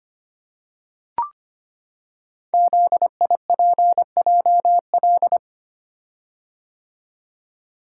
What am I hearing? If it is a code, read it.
Morse code, heard as a keyed tone, about 25 wpm, 700 Hz. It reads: ZIPJL
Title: Code Groups - 5 Letter:Numbers - 25wpm